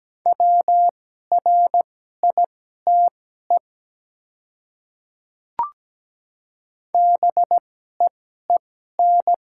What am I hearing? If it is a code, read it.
Morse code, heard as a keyed tone, about 17 wpm, spaced as for 12 wpm, 700 Hz. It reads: WRITE BEEN
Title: 2x Character Spacing - Top 100 Words - 17wpm